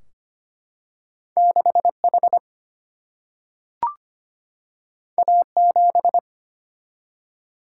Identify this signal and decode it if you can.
Morse code, heard as a keyed tone, about 25 wpm, 700 Hz. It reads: 6H A7
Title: Single-Letter Single-Number 25wpm